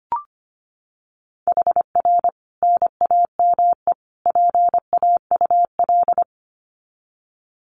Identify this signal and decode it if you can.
Morse code, heard as a keyed tone, about 25 wpm, 700 Hz. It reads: HR NAME PAUL
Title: Common QSO Phrases 25wpm